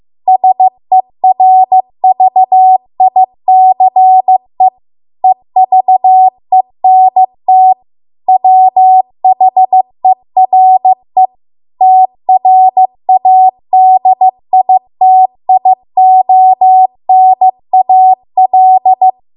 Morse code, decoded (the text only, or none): SERVICE EVENT WHERE TRADITIONAL